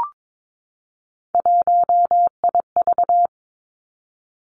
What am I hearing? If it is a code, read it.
Morse code, heard as a keyed tone, about 22 wpm, 700 Hz. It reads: 1IV
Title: Three Challenging Letter-Number Combinations 22wpm